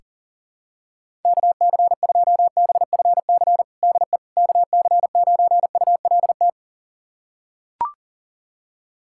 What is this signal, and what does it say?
Morse code, heard as a keyed tone, about 40 wpm, 700 Hz. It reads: KC2BFC DE KC9ULT
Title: USA Callsign DE Callsign - 6 Characters - 40wpm